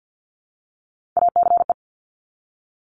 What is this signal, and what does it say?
Morse code, heard as a keyed tone, about 50 wpm, 690 Hz. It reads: ACE
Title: Top Three Letter Combinations in Words 50wpm